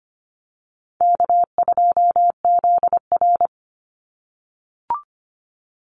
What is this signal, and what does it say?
Morse code, heard as a keyed tone, about 25 wpm, 700 Hz. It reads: K2ZR
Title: Call Signs - 25wpm